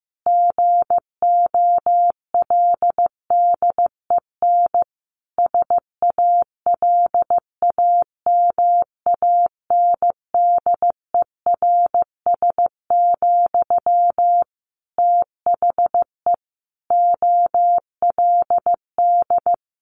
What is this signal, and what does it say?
Morse code, heard as a keyed tone, about 15 wpm, 700 Hz. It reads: GOLDEN SALAMANDERS, THE OLD
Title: A Sound Of Thunder 15wpm